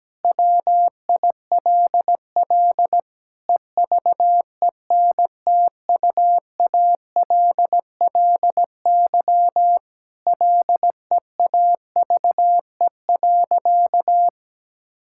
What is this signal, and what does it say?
Morse code, heard as a keyed tone, about 17 wpm, 700 Hz. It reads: WILL EVENTUALLY LEAVE.
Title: Sentences from Top 1000 Words - Review Part 6 - 17wpm